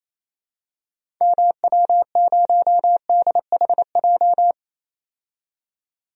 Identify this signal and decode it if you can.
Morse code, heard as a keyed tone, about 28 wpm, 700 Hz. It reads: MW0DHJ
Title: DXCC-100 Call Signs - 6 Characters - 28wpm